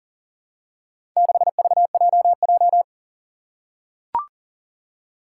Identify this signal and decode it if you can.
Morse code, heard as a keyed tone, about 40 wpm, 700 Hz. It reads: 6VJJ
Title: Four Challenging Letter-Number Combinations 40wpm